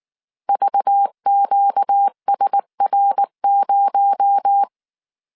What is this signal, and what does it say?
Morse code, heard as a keyed tone, about 19 wpm, 800 Hz. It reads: VQSR0
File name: CW / DET / 1 6 1500C S C 18wpm 2356